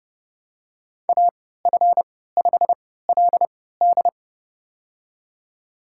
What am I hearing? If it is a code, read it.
Morse code, heard as a keyed tone, about 32 wpm, spaced as for 17 wpm, 700 Hz. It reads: AF5LD
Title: USA Call Signs - 3x Character Spacing - 30wpm